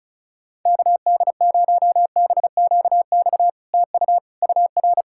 Morse code, heard as a keyed tone, about 38 wpm, 700 Hz. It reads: KD0BQX TU UR
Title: SOTA QSO Practice 35wpm